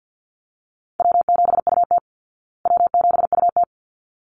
Morse code, heard as a keyed tone, about 50 wpm, 700 Hz. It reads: W7UT W7UT
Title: Easy Call Signs Repeated 2x - 50wpm